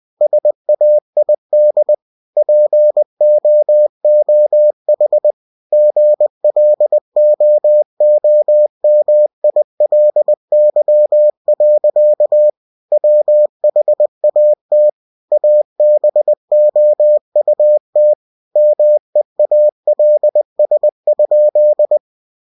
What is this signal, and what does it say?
Morse code, heard as a keyed tone, about 20 wpm, 600 Hz. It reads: SAID POOH GLOOMILY. WHAT ABOUT MEALS?
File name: cw-books / winnie-the-pooh / winnie-20wpm-page-0019